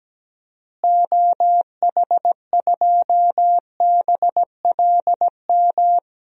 Morse code, heard as a keyed tone, about 17 wpm, 700 Hz. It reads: OH2BLM
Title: DXCC-100 Call Signs - 6 Characters - Repeated 2x - 17wpm